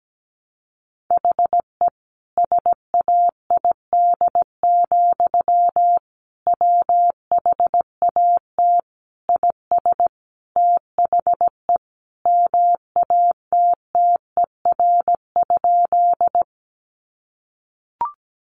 Morse code, heard as a keyed tone, about 17 wpm, 700 Hz. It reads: HE SAID, WHAT IS THE MATTER?
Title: Sentences from Top 700 Words 17wpm